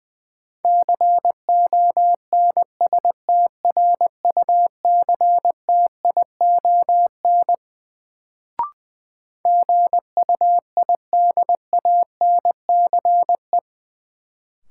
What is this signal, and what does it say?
Morse code, heard as a keyed tone, about 20 wpm, 700 Hz. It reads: CONSTRUCTION GUIDANCE
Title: Sentences from Top 1600 Words - New Word Drill Alt - 20wpm